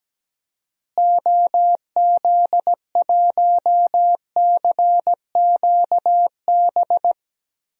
Morse code, heard as a keyed tone, about 17 wpm, 700 Hz. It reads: OZ1CQB
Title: DXCC-100 Call Signs - 6 Characters - Repeated 3x - 17wpm